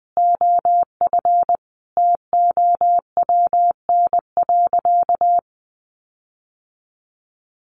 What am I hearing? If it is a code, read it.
Morse code, heard as a keyed tone, about 20 wpm, 700 Hz. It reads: OF TOWN.
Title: Sentences from Top 1500 Words 20wpm